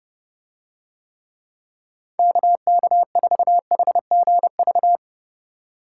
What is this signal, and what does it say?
Morse code, heard as a keyed tone, about 32 wpm, 700 Hz. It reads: KK4HGV